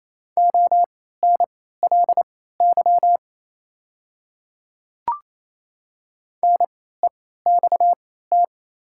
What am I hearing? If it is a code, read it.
Morse code, heard as a keyed tone, about 28 wpm, spaced as for 16 wpm, 700 Hz. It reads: ONLY NEXT
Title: 3x Character Spacing - Top 300 Words - 28wpm